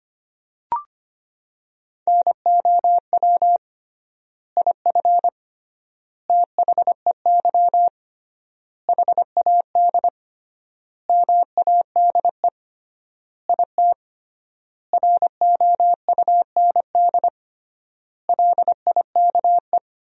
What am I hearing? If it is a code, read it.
Morse code, heard as a keyed tone, about 25 wpm, 700 Hz. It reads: NOW IF THEY HAD MADE IT ROUND LIKE
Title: 3x Word-Spacing - Sentences from Top 500 Words - 25wpm